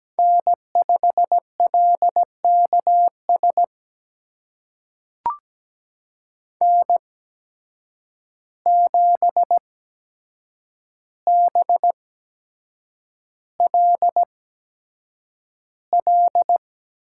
Morse code, heard as a keyed tone, about 17 wpm, spaced as for 7 wpm, 700 Hz. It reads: N5LKS N7BLL
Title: USA Call Signs - 5 Characters - 8x Character Spacing - 17wpm